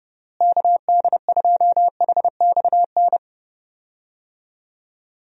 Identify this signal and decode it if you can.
Morse code, heard as a keyed tone, about 30 wpm, 700 Hz. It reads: KD2HXN
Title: USA Call Signs - 6 Characters - Repeated 3x - 30wpm